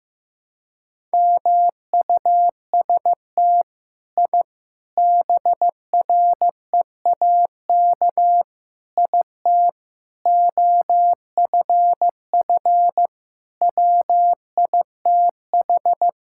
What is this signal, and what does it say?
Morse code, heard as a keyed tone, about 16 wpm, 700 Hz. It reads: MUST I BREAK IT OFF WITH
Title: Sentences from Top 1000 Words - Review Part 6 - 15wpm